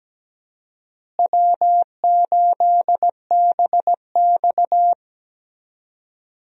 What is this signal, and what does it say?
Morse code, heard as a keyed tone, about 17 wpm, 700 Hz. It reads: W8BX
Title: USA Call Signs - 4 Characters - Repeated 3x - 17wpm